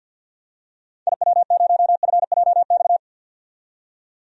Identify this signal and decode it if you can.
Morse code, heard as a keyed tone, about 47 wpm, 700 Hz. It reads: IW0FJX
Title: DXCC-100 Call Signs - 6 Characters - 50wpm